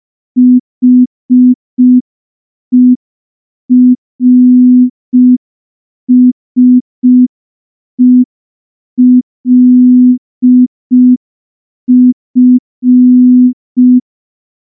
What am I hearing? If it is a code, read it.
Morse code, heard as a keyed tone, about 5 wpm, 250 Hz. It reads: HERSELF
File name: morse / morse-training-05WPM